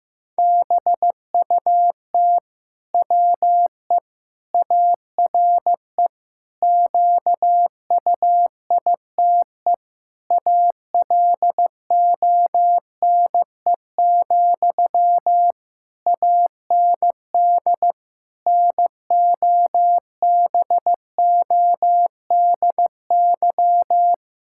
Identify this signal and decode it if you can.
Morse code, heard as a keyed tone, about 15 wpm, 700 Hz. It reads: BUT WE ARE QUITE ALONE, AND NOBODY
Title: Sentences from Top 1600 Words 15wpm